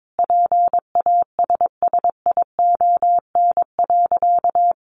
Morse code, heard as a keyed tone, about 22 wpm, 700 Hz. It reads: PASSION.
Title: Sentences from Top 1300 Words 22wpm